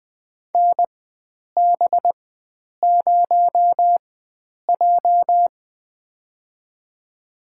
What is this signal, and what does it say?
Morse code, heard as a keyed tone, about 20 wpm, spaced as for 9 wpm, 700 Hz. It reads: NB0J
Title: USA Call Signs - 4 Characters - 4x Character Spacing - 20wpm